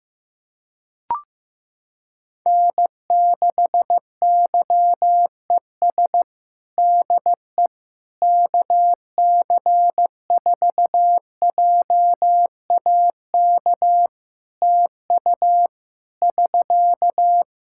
Morse code, heard as a keyed tone, about 16 wpm, 700 Hz. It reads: N6YES DE KC4JAK TU <SK>
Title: K1USN SST QSO Practice 15wpm